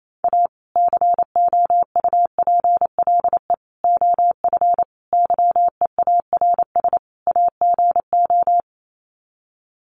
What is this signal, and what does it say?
Morse code, heard as a keyed tone, about 28 wpm, 700 Hz. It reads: A COUPLE OF YEARS AGO
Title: Sets of 5 Words 28wpm